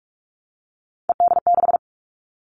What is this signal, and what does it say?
Morse code, heard as a keyed tone, about 46 wpm, 700 Hz. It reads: ED6
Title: Callsign Prefixes for Top 50 DXCC - 45wpm